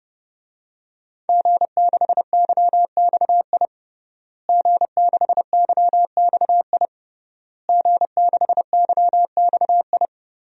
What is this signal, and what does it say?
Morse code, heard as a keyed tone, about 30 wpm, 700 Hz. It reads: G6YXI G6YXI G6YXI
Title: DXCC-100 Call Signs - 5 Characters - Repeated 3x - 30wpm